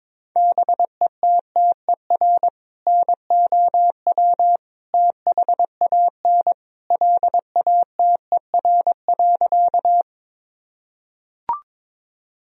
Morse code, heard as a keyed tone, about 22 wpm, 700 Hz. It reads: BETTER NOW THAN LATER.
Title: Sentences from Top 1000 Words - Review Part 4 - 22wpm